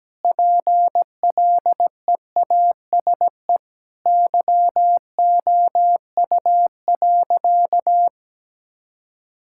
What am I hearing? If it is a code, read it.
Morse code, heard as a keyed tone, about 17 wpm, 700 Hz. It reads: PLEASE YOU.
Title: Sentences from Top 1600 Words 17wpm